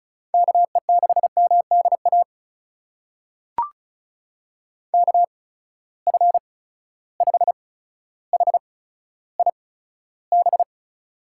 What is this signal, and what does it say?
Morse code, heard as a keyed tone, about 34 wpm, spaced as for 15 wpm, 700 Hz. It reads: KE6MDA KF5HIB
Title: USA Call Signs - 6 Characters - 8x Character Spacing - 35wpm